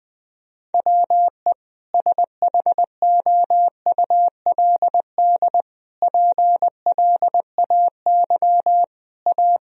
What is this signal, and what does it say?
Morse code, heard as a keyed tone, about 21 wpm, 700 Hz. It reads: WE SHOULD PLAY A
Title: Sentences from Top 1000 Words - Review Part 3 - 20wpm